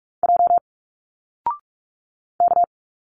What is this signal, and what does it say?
Morse code, heard as a keyed tone, about 46 wpm, 700 Hz. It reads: J K
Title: Single Letters 45wpm